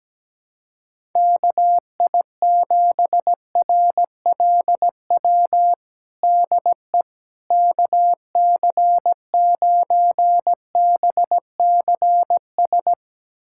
Morse code, heard as a keyed tone, about 17 wpm, 700 Hz. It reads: KI7RLW DE KC9BCS
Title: USA Callsign DE Callsign - 6 Characters - 17wpm